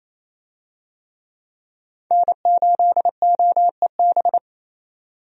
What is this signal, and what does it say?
Morse code, heard as a keyed tone, about 28 wpm, 700 Hz. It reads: N8OEB
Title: USA Call Signs - 5 Characters - 28wpm